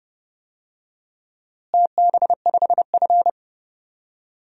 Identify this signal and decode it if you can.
Morse code, heard as a keyed tone, about 31 wpm, 700 Hz. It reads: TB5F
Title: DXCC-100 Call Signs - 4 Characters - Repeated 3x - 30wpm